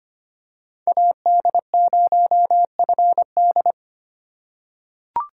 Morse code, heard as a keyed tone, about 25 wpm, 690 Hz. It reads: AD0FD
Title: USA Call Signs - 5 Characters - 6x Character Spacing - 25wpm